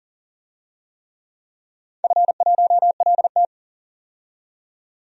Morse code, heard as a keyed tone, about 42 wpm, 700 Hz. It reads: F1LT
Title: DXCC-100 Call Signs - 4 Characters - 40wpm